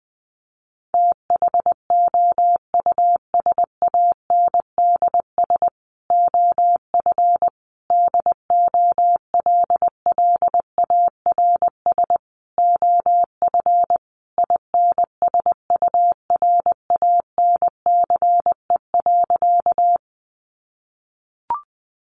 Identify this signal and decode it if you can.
Morse code, heard as a keyed tone, about 20 wpm, 700 Hz. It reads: THOUSANDS OF DOLLARS OF INSURANCE.
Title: A Sound Of Thunder 20wpm